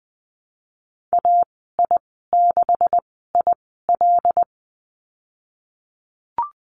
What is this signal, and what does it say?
Morse code, heard as a keyed tone, about 20 wpm, spaced as for 15 wpm, 700 Hz. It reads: AI6IL
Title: 154 Lesson 25g - Letter p - Call Signs 20wpm